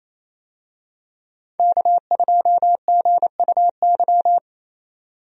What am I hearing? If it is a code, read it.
Morse code, heard as a keyed tone, about 27 wpm, 700 Hz. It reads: K2GUY
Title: USA Call Signs - 5 Characters - Repeated 3x - 28wpm